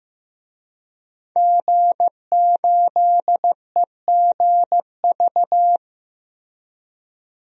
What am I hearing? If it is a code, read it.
Morse code, heard as a keyed tone, about 15 wpm, 700 Hz. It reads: G8EGV
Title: DXCC-100 Call Signs - 5 Characters - Repeated 2x - 15wpm